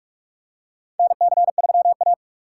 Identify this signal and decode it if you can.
Morse code, heard as a keyed tone, about 46 wpm, 700 Hz. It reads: NC3A
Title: USA Call Signs - 6x Character Spacing - 45wpm